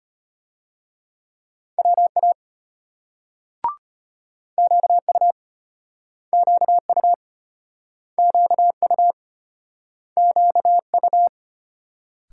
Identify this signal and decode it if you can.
Morse code, heard as a keyed tone, about 34 wpm, 700 Hz. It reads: WA QU QU QU QU